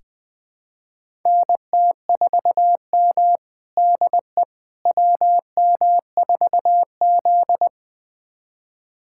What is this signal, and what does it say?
Morse code, heard as a keyed tone, about 20 wpm, 700 Hz. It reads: NT4M DE WM4Z
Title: USA Callsign DE Callsign - 4 Characters - 20wpm